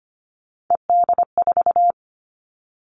Morse code, heard as a keyed tone, about 25 wpm, 700 Hz. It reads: ED4
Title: All Cut Numbers - 3-Digit 25wpm